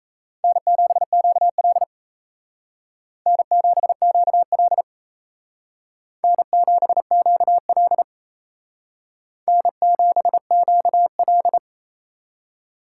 Morse code, heard as a keyed tone, about 36 wpm, 700 Hz. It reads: N7QL N7QL N7QL N7QL